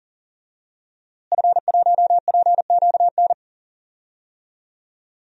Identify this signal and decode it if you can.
Morse code, heard as a keyed tone, about 40 wpm, 700 Hz. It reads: F1PQN
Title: DXCC-100 Call Signs - 5 Characters - Repeated 3x - 40wpm